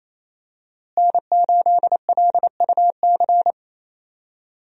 Morse code, heard as a keyed tone, about 28 wpm, 700 Hz. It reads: N8LUC
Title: USA Call Signs - 5 Characters - Repeated 2x - 28wpm